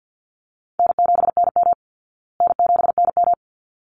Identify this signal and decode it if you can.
Morse code, heard as a keyed tone, about 50 wpm, 700 Hz. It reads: N7NM N7NM
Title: Call Signs Repeated 3x - 50wpm